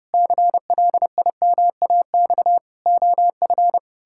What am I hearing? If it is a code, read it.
Morse code, heard as a keyed tone, about 30 wpm, 700 Hz. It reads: CLIMAX OF
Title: Pangram Challenge v2 - 30wpm